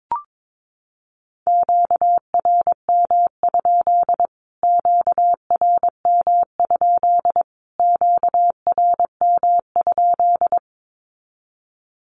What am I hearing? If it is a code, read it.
Morse code, heard as a keyed tone, about 22 wpm, 700 Hz. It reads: QRM? QRM? QRM?
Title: Q-Codes Repeated 3x 22wpm